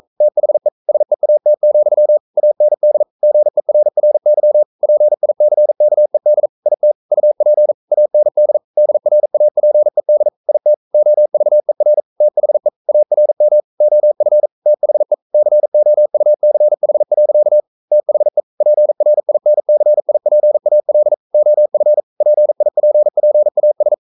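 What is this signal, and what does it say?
Morse code, heard as a keyed tone, about 42 wpm, 600 Hz. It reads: THE SEAT, AND GERRY PICKED IT UP AND DRAPED IT OVER THE ARM OF THE COUCH. THE PRINCIPAL OF PIPPAS
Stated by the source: Morse code practice at 41 wpm, from CW med Gnister.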